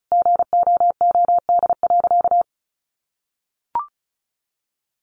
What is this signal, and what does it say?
Morse code, heard as a keyed tone, about 34 wpm, 700 Hz. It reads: GOOD.
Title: Sentences from Top 600 Words 35wpm